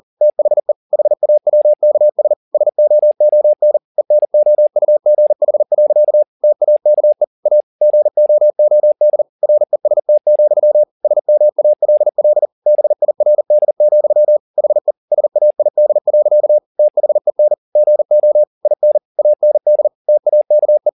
Morse code, heard as a keyed tone, about 40 wpm, 600 Hz. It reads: THE HAWKS SOON ENOUGH. TAKE A GOOD REST, SMALL BIRD, HE SAID. THEN GO IN AND TAKE